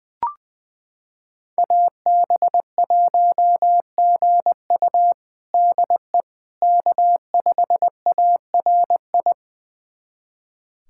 Morse code, heard as a keyed tone, about 21 wpm, 700 Hz. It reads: AB1GU DE K5ARI
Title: USA Callsign DE Callsign - 5 Characters - 20wpm